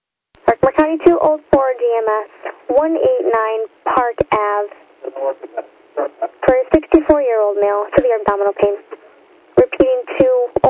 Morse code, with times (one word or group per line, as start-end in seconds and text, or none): none